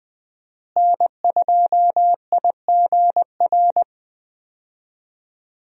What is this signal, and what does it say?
Morse code, heard as a keyed tone, about 20 wpm, 700 Hz. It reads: N2IGR
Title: USA Call Signs - 5 Characters - Repeated 2x - 20wpm